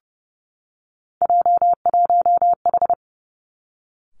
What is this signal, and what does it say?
Morse code, heard as a keyed tone, about 30 wpm, 700 Hz. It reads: J1H
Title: Three Challenging Letter-Number Combinations 30wpm